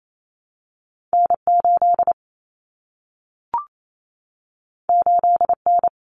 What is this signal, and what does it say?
Morse code, heard as a keyed tone, about 28 wpm, 700 Hz. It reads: N8 8N
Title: Common Cut Numbers - 2-Digit 28wpm